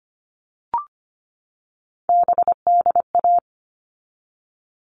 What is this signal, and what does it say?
Morse code, heard as a keyed tone, about 25 wpm, 700 Hz. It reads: BDA
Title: All Cut Numbers - 3-Digit 25wpm